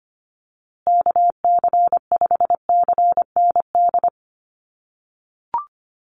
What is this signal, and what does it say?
Morse code, heard as a keyed tone, about 25 wpm, 700 Hz. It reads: KC5CND
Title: USA Call Signs Repeated 2x 25wpm